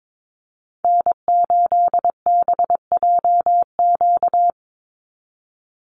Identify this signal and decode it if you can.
Morse code, heard as a keyed tone, about 22 wpm, 700 Hz. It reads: N8BJQ
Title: Call Signs Repeated 3x - 22wpm